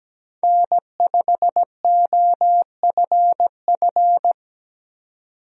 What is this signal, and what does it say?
Morse code, heard as a keyed tone, about 17 wpm, 700 Hz. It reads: N5OFF
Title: USA Call Signs - 5 Characters - 17wpm